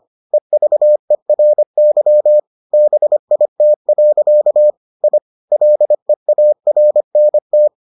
Morse code, heard as a keyed tone, about 25 wpm, 600 Hz. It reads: EVERY BIT. I LEARNT